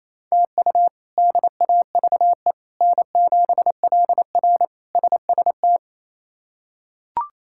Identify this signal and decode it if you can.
Morse code, heard as a keyed tone, about 28 wpm, 700 Hz. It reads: TU DAVE N7LR SST
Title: K1USN SST QSO Practice - 28wpm